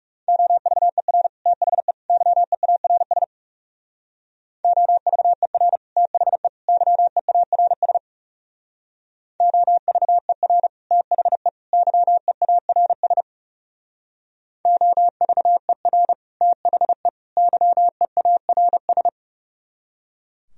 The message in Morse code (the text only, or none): OVER THE YEARS OVER THE YEARS OVER THE YEARS OVER THE YEARS